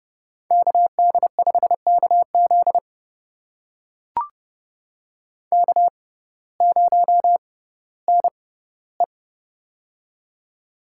Morse code, heard as a keyed tone, about 30 wpm, spaced as for 17 wpm, 700 Hz. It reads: KD5KZ K0NE
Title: USA Call Signs - 6x Character Spacing - 30wpm